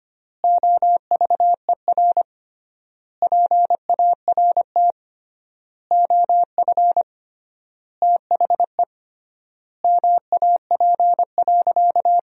Morse code, heard as a keyed tone, about 25 wpm, 700 Hz. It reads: OVER PART OF THE MAP.
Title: 3x Word-Spacing - Sentences from Top 400 Words - 25wpm